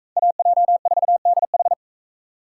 Morse code, heard as a keyed tone, about 39 wpm, 700 Hz. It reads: AJ4DH